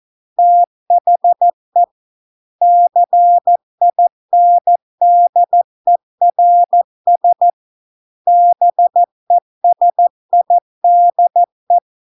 Morse code, heard as a keyed tone, about 14 wpm, 700 Hz. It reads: THE CINDERS BESIDE